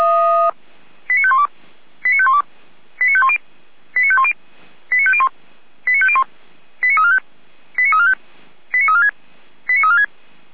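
Five-tone selective call (ZVEI), heard as quick runs of five tones, tones of about 70 ms. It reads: 89512 89512 89510 89510 89501 89501 89345 89345 89346 89346
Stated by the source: ZVEI five-tone selective call received by SDR.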